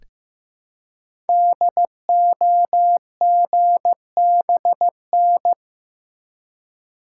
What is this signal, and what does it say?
Morse code, heard as a keyed tone, about 15 wpm, 700 Hz. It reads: DOGBN
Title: Code Groups - 5 Letters - 15wpm